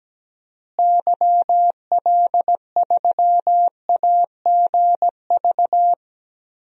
Morse code, heard as a keyed tone, about 17 wpm, 700 Hz. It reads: YL3AGV
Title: DXCC-100 Call Signs - 6 Characters - Repeated 2x - 17wpm